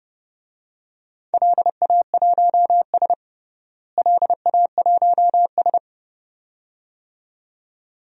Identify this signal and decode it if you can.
Morse code, heard as a keyed tone, about 30 wpm, 700 Hz. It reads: LA1S LA1S
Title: DXCC-100 Call Signs - 4 Characters - Repeated 2x - 30wpm